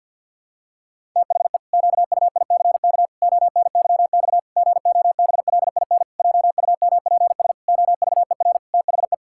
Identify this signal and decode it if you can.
Morse code, heard as a keyed tone, about 51 wpm, 700 Hz. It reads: THE QUICK ONYX GOBLIN JUMPS OVER THE
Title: Pangram Challenge v2 - 50wpm